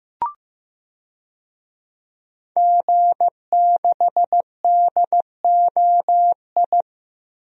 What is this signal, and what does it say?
Morse code, heard as a keyed tone, about 15 wpm, 700 Hz. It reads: G6DOI
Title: DXCC-100 Call Signs - 5 Characters - Repeated 3x - 15wpm